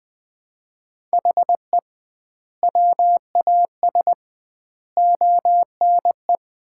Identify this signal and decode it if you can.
Morse code, heard as a keyed tone, about 20 wpm, 700 Hz. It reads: HE WAS ONE
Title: 2x Word-Spacing - Sets of 5 Words - 20wpm